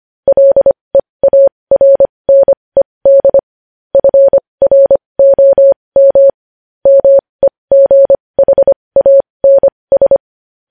Morse code, heard as a keyed tone, about 25 wpm, spaced as for 20 wpm, 550 Hz. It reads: LEARNED FROM MEGHANS
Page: news-headlines-in-morse-code-at-20-wpm